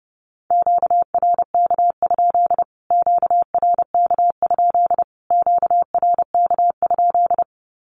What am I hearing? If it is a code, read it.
Morse code, heard as a keyed tone, about 28 wpm, 700 Hz. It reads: QRK? QRK? QRK?
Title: Q-Codes Repeated 3x 30wpm